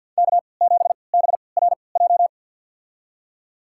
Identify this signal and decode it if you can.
Morse code, heard as a keyed tone, about 53 wpm, spaced as for 29 wpm, 700 Hz. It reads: K7BRJ
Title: USA Call Signs - 5 Characters - 3x Character Spacing - 50wpm